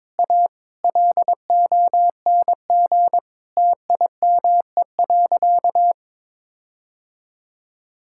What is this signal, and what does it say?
Morse code, heard as a keyed tone, about 22 wpm, 700 Hz. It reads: A LONG TIME.
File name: Sentences with 7-Word Phrases - 22wpm